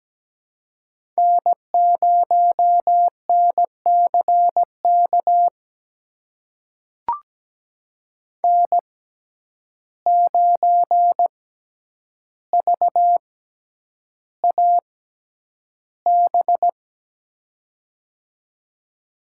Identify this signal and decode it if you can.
Morse code, heard as a keyed tone, about 17 wpm, spaced as for 9 wpm, 700 Hz. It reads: N0NCK N9VAB
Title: USA Call Signs - 5 Characters - 6x Character Spacing - 17wpm